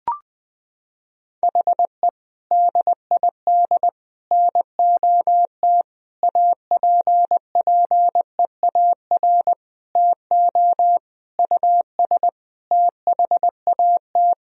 Morse code, heard as a keyed tone, about 21 wpm, 700 Hz. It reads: HE DID NOT APPEAR TO US THAT
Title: Sentences from Top 1000 Words - Review Part 2 - 20wpm